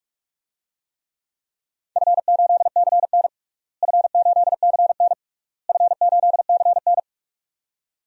Morse code, heard as a keyed tone, about 42 wpm, 700 Hz. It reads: F8CN F8CN F8CN
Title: DXCC-100 Call Signs - 4 Characters - Repeated 3x - 45wpm